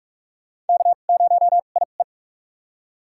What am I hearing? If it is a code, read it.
Morse code, heard as a keyed tone, about 46 wpm, spaced as for 33 wpm, 700 Hz. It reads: K0IE
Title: USA Call Signs - 2x Character Spacing - 45wpm